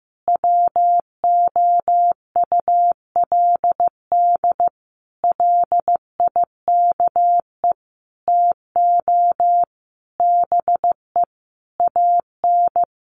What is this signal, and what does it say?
Morse code, heard as a keyed tone, about 15 wpm, 700 Hz. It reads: WOULD LIKE TO BE AN
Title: Sentences from Top 1500 Words 15wpm